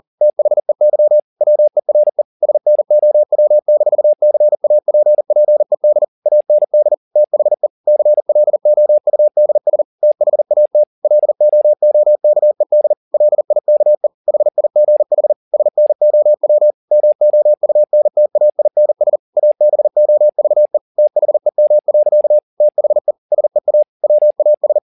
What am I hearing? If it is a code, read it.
Morse code, heard as a keyed tone, about 40 wpm, 600 Hz. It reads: THEY WERE SNOW-CAPPED AND THE CLOUDS THAT LOOKED LIKE HIGH SNOW MOUNTAINS ABOVE THEM. THE SEA WAS